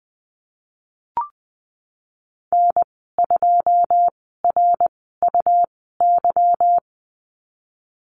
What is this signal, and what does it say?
Morse code, heard as a keyed tone, about 19 wpm, spaced as for 14 wpm, 700 Hz. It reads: N2RUY
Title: 167 Lesson 27g - Number 7 - Call Signs 20wpm